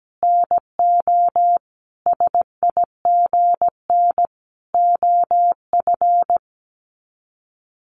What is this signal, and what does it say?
Morse code, heard as a keyed tone, about 17 wpm, 700 Hz. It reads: NO SIGN OF
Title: Sets of 5 Words Encore2 17wpm